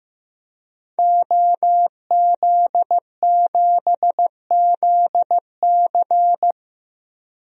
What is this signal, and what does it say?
Morse code, heard as a keyed tone, about 16 wpm, 700 Hz. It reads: OZ7ZC
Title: DXCC-100 Call Signs - 5 Characters - Repeated 2x - 15wpm